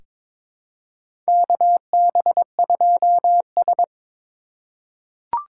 Morse code, heard as a keyed tone, about 22 wpm, 700 Hz. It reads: KB2S
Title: K1USN SST QSO Practice - 22wpm